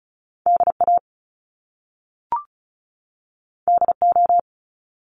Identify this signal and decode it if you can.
Morse code, heard as a keyed tone, about 36 wpm, 700 Hz. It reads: DA DO
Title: Top Two Letter Combinations in Words 35wpm